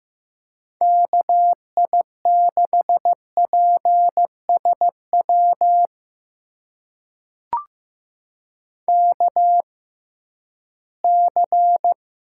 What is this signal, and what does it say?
Morse code, heard as a keyed tone, about 15 wpm, spaced as for 11 wpm, 700 Hz. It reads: KI6PSW KC
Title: USA Call Signs - 6 Characters - 6x Character Spacing - 15wpm